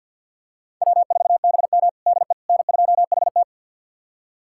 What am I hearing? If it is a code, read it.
Morse code, heard as a keyed tone, about 50 wpm, 700 Hz. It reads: W4BM DE N2HT
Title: USA Callsign DE Callsign - 4 Characters - 50wpm